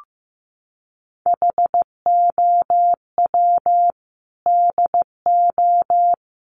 Morse code, heard as a keyed tone, about 15 wpm, 700 Hz. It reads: HOW DO
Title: Sentences from Top 1300 Words 15wpm